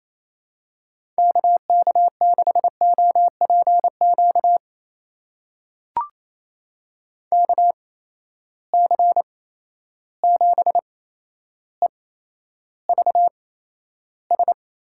Morse code, heard as a keyed tone, about 28 wpm, spaced as for 12 wpm, 700 Hz. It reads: KK6OPQ KC7EVS
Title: USA Call Signs - 6 Characters - 8x Character Spacing - 28wpm